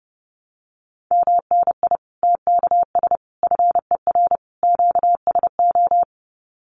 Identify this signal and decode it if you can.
Morse code, heard as a keyed tone, about 30 wpm, 700 Hz. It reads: MNI TKS FER QSO
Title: Common QSO Phrases 30wpm